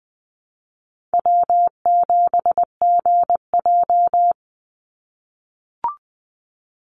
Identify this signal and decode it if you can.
Morse code, heard as a keyed tone, about 20 wpm, 700 Hz. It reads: W7GJ
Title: NA SOTA Chaser Callsigns 20wpm